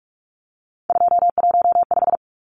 Morse code, heard as a keyed tone, about 44 wpm, 700 Hz. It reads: 215
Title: Numbers - 3-Digit 45wpm